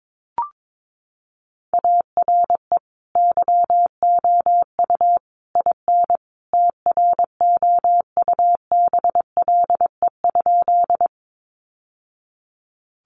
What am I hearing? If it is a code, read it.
Morse code, heard as a keyed tone, about 22 wpm, 700 Hz. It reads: ARE YOU IN TROUBLE?
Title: Sentences from Top 1200 Words 22wpm